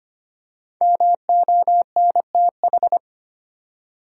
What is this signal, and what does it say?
Morse code, heard as a keyed tone, about 25 wpm, 700 Hz. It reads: MONTH
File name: 5 Letter Words - First 3 Letters Given - 25wpm